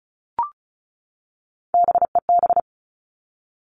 Morse code, heard as a keyed tone, about 36 wpm, 700 Hz. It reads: BEB
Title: Three Challenging Letter-Number Combinations 35wpm